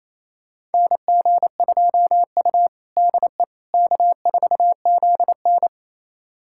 Morse code, heard as a keyed tone, about 28 wpm, 700 Hz. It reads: NG2U DE K4ZN
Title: USA Callsign DE Callsign - 4 Characters - 28wpm